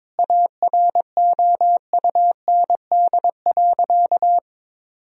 Morse code, heard as a keyed tone, about 21 wpm, 700 Hz. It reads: AROUND.
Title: Sentences from Top 1000 Words - Review Part 3 - 22wpm